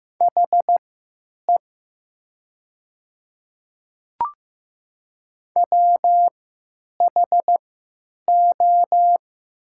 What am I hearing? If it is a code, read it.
Morse code, heard as a keyed tone, about 15 wpm, spaced as for 8 wpm, 700 Hz. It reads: HE WHO
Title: 3x Character Spacing - Top 100 Words - 15wpm